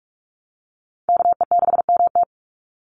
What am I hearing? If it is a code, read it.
Morse code, heard as a keyed tone, about 46 wpm, 690 Hz. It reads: KE6MT
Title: NA SOTA Chaser Callsigns 45wpm